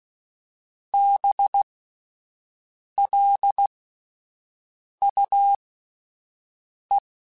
Morse code, heard as a keyed tone, about 16 wpm, spaced as for 5 wpm, 800 Hz. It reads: BLUE